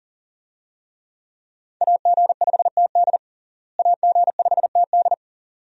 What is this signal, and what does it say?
Morse code, heard as a keyed tone, about 37 wpm, 700 Hz. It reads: AG5TD AG5TD
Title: USA Call Signs - 5 Characters - Repeated 2x - 40wpm